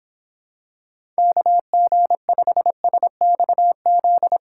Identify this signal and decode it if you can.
Morse code, heard as a keyed tone, about 26 wpm, 700 Hz. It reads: KG5SXZ